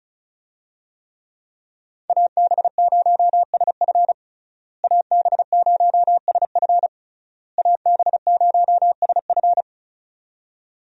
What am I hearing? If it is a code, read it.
Morse code, heard as a keyed tone, about 34 wpm, 700 Hz. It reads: AB0SF AB0SF AB0SF
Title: USA Call Signs - 5 Characters - Repeated 3x - 35wpm